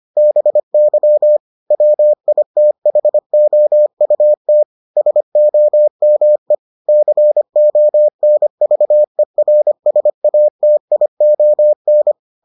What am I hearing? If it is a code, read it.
Morse code, heard as a keyed tone, about 25 wpm, 600 Hz. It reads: BY WITHOUT SOME CONVERSATION